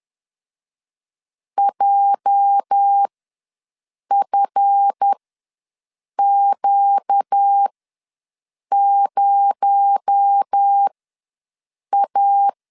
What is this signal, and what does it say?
Morse code, heard as a keyed tone, about 10 wpm, spaced as for 6 wpm, 800 Hz. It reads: JFQ0A
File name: LEZIONI / 8 YJ45 / 8 YJ ws 10wpm 4043